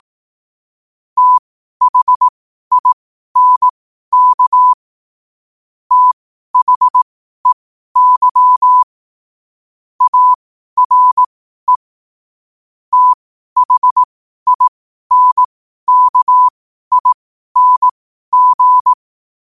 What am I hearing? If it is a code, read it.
Morse code, heard as a keyed tone, about 18 wpm, spaced as for 12 wpm, 1000 Hz. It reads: THINK THEY ARE THINKING